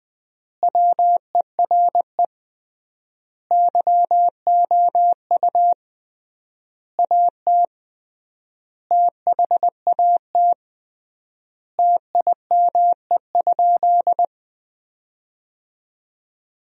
Morse code, heard as a keyed tone, about 20 wpm, 700 Hz. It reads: WERE YOU AT THAT TIME?
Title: 3x Word-Spacing - Sentences from Top 200 Words - 20wpm